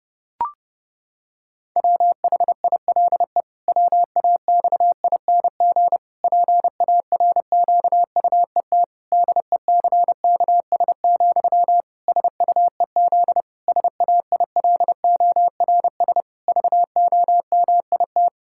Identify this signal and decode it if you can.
Morse code, heard as a keyed tone, about 32 wpm, 700 Hz. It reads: WHILE WAXING PARQUET DECKS, SUEZ SAILORS VOMIT
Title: Pangram Challenge v2 - 30wpm